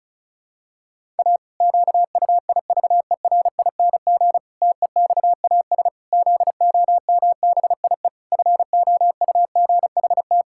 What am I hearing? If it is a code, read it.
Morse code, heard as a keyed tone, about 36 wpm, 700 Hz. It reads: A QUIVERING TEXAS ZOMBIE FOUGHT
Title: Pangram Challenge v2 - 35wpm